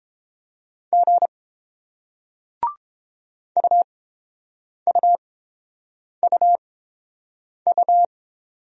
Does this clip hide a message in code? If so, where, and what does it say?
Yes: Morse code from 0.9 to 8.1 s, heard as a keyed tone, about 30 wpm, spaced as for 7 wpm, 700 Hz. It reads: G UUUU